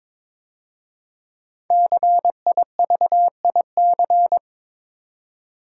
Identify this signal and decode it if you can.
Morse code, heard as a keyed tone, about 22 wpm, 700 Hz. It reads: CIVIC
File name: 5 Letter Words - First 3 Letters Given - 22wpm